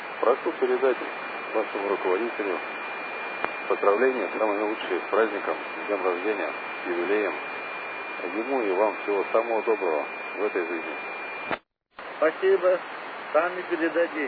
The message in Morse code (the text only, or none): none